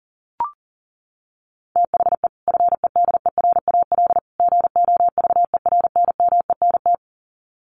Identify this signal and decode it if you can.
Morse code, heard as a keyed tone, about 40 wpm, 700 Hz. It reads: THE FEDERAL GOVERNMENT
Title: Sets of 3 Words 40wpm